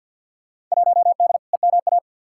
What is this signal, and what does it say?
Morse code, heard as a keyed tone, about 52 wpm, 700 Hz. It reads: 1D EMA
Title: Field Day Exchanges - 50wpm